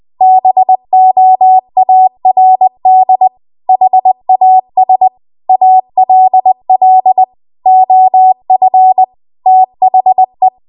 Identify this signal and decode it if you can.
Morse code, heard as a keyed tone, about 20 wpm, 750 Hz. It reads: BOARD HAS ALL OF THE